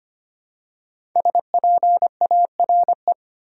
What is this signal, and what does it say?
Morse code, heard as a keyed tone, about 25 wpm, 700 Hz. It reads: SPARE
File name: 5 Letter Words - First 3 Letters Given - 25wpm